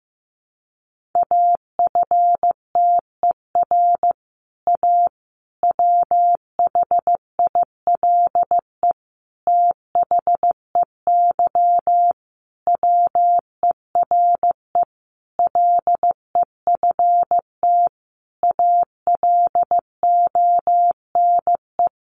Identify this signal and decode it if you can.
Morse code, heard as a keyed tone, about 15 wpm, 700 Hz. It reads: AFTER A WHILE THEY WERE LEFT ALONE
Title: Sentences from Top 1000 Words 15wpm